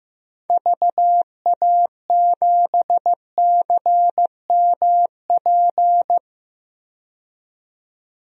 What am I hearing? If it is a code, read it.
Morse code, heard as a keyed tone, about 15 wpm, 700 Hz. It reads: VA7CMP
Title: DXCC-100 Call Signs - 6 Characters - 15wpm